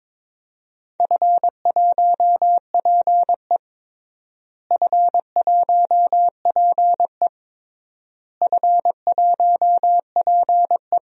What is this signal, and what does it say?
Morse code, heard as a keyed tone, about 22 wpm, 700 Hz. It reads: F1PE F1PE F1PE
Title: DXCC-100 Call Signs - 4 Characters - Repeated 3x - 22wpm